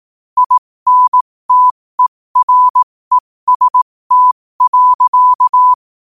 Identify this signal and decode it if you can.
Morse code, heard as a keyed tone, about 18 wpm, spaced as for 15 wpm, 1000 Hz. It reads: INTEREST.
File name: qotdfiles / 15wpm